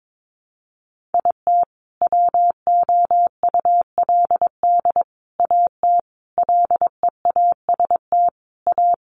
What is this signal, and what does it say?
Morse code, heard as a keyed tone, about 22 wpm, 700 Hz. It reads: IT WOULD AT LEAST A
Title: Sentences from Top 700 Words 22wpm